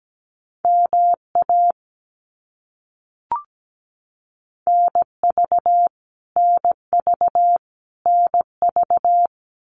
Morse code, heard as a keyed tone, about 17 wpm, 700 Hz. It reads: MA NV NV NV
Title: US States Repeated 3x 17wpm